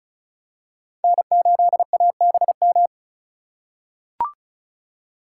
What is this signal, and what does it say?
Morse code, heard as a keyed tone, about 34 wpm, 700 Hz. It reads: N8ABM
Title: USA Call Signs - 5 Characters - 4x Character Spacing - 35wpm